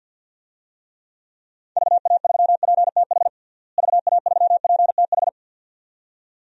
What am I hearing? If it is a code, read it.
Morse code, heard as a keyed tone, about 51 wpm, 700 Hz. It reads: VA3PTH VA3PTH
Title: DXCC-100 Call Signs - 6 Characters - Repeated 2x - 50wpm